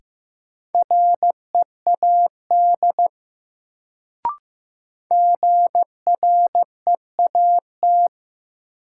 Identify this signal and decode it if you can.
Morse code, heard as a keyed tone, about 15 wpm, 700 Hz. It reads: READ GREAT
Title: Sentences from Top 200 Words - New Word Drill - 15wpm